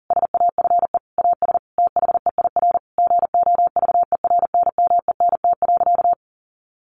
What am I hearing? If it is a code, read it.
Morse code, heard as a keyed tone, about 40 wpm, 700 Hz. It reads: SAFE AS THEIR GOVERNMENT.
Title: Sentences from Top 1500 Words 40wpm